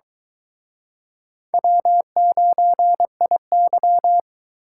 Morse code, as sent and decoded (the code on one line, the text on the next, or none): .-- ----. .. -.--
W9IY